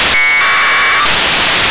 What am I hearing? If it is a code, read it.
Packet radio (AX.25, as in APRS).